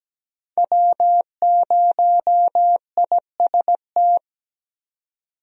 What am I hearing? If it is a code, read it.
Morse code, heard as a keyed tone, about 17 wpm, 700 Hz. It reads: W0IST
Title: USA Call Signs - 5 Characters - 17wpm